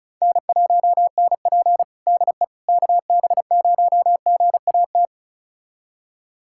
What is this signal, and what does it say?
Morse code, heard as a keyed tone, about 36 wpm, 700 Hz. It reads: N1NP DE KB0GAT
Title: USA Callsign DE Callsign 35wpm